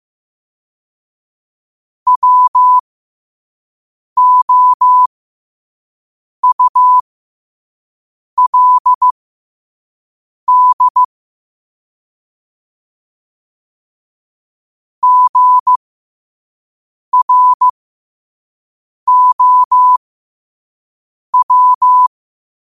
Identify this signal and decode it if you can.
Morse code, heard as a keyed tone, about 15 wpm, spaced as for 5 wpm, 1000 Hz. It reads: WOULD GROW
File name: qotdfiles / 5wpm